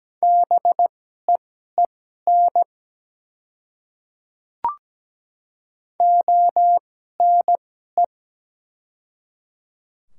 Morse code, heard as a keyed tone, about 17 wpm, spaced as for 12 wpm, 700 Hz. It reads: BEEN ONE
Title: 2x Character Spacing - Top 100 Words - 17wpm